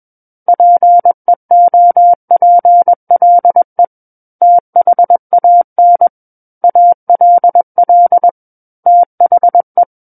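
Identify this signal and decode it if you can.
Morse code, heard as a keyed tone, about 21 wpm, 700 Hz. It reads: PEOPLE THAN ALL THE
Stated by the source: Morse code practice reading of a book chapter at 21 words per minute.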